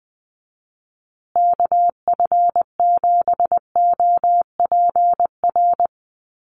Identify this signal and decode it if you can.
Morse code, heard as a keyed tone, about 20 wpm, 700 Hz. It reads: KF7OPR
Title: USA Call Signs Repeated 2x 20wpm